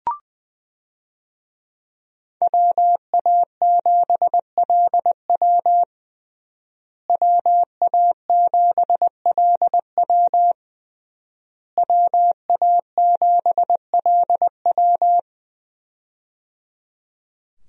Morse code, heard as a keyed tone, about 21 wpm, 700 Hz. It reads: WA7LW WA7LW WA7LW
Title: USA Call Signs - 5 Characters - Repeated 3x - 20wpm